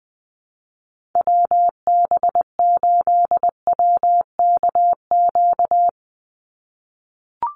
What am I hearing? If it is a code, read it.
Morse code, heard as a keyed tone, about 19 wpm, 700 Hz. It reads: WB8WKQ
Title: Call Signs - 20wpm